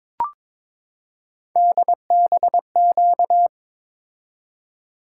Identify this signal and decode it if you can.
Morse code, heard as a keyed tone, about 22 wpm, 700 Hz. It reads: DBQ
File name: practice / 22wpm / Three Letter-Number Combinations - 22wpm